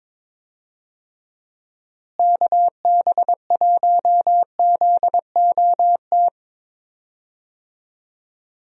Morse code, heard as a keyed tone, about 22 wpm, 700 Hz. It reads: KB1ZOT
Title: USA Call Signs - 6 Characters - Repeated 3x - 22wpm